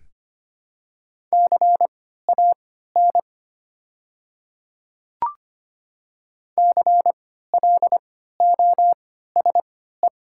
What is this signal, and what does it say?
Morse code, heard as a keyed tone, about 25 wpm, spaced as for 14 wpm, 700 Hz. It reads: CAN CLOSE
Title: 3x Character Spacing - Top 300 Words - 25wpm